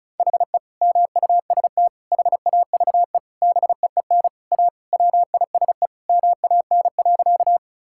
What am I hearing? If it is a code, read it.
Morse code, heard as a keyed tone, about 36 wpm, 700 Hz. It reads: HE MUST HAVE BEEN A WISE MAN.
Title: Sentences from Top 1600 Words 35wpm